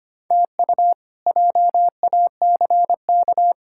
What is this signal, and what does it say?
Morse code, heard as a keyed tone, about 25 wpm, 700 Hz. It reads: TU JACK
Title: K1USN SST QSO Practice - 25wpm